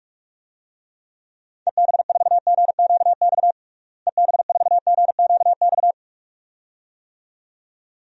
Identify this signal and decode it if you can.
Morse code, heard as a keyed tone, about 42 wpm, 700 Hz. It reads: EB4GQX EB4GQX
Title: DXCC-100 Call Signs - 6 Characters - Repeated 2x - 45wpm